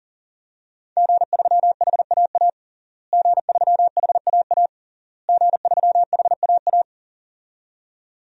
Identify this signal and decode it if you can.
Morse code, heard as a keyed tone, about 40 wpm, 700 Hz. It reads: G3HAA G3HAA G3HAA
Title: DXCC-100 Call Signs - 5 Characters - Repeated 3x - 40wpm